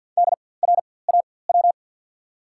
Morse code, heard as a keyed tone, about 52 wpm, spaced as for 23 wpm, 700 Hz. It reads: DRAW
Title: 4x Character Spacing - Top 500 Words - 50wpm